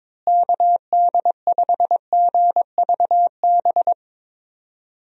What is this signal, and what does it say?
Morse code, heard as a keyed tone, about 22 wpm, 700 Hz. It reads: KD5GVB
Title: USA Call Signs - 6 Characters - Repeated 2x - 22wpm